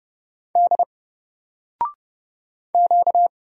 Morse code, heard as a keyed tone, about 31 wpm, 700 Hz. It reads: D Q
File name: Single-Letter-Number-30wpm